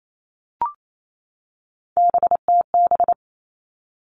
Morse code, heard as a keyed tone, about 28 wpm, 700 Hz. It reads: BTB
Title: All Cut Numbers - 3-Digit 28wpm